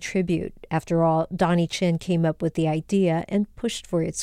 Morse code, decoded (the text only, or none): none